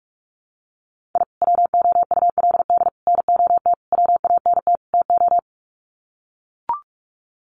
Morse code, heard as a keyed tone, about 44 wpm, 700 Hz. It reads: I WOULD NOT WANT TO
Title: Sets of 5 Words 45wpm